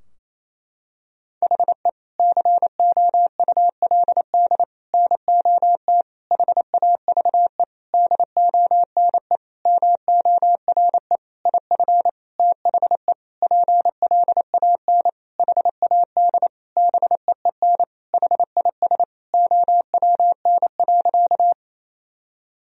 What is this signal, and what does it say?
Morse code, heard as a keyed tone, about 28 wpm, 700 Hz. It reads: HE COULD NOT HAVE DONE MORE IF THE PLAN HAD BEEN HIS OWN.
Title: Sentences from Top 1000 Words - Review Part 2 - 28wpm